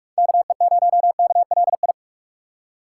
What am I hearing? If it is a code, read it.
Morse code, heard as a keyed tone, about 44 wpm, 700 Hz. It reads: KE0KLI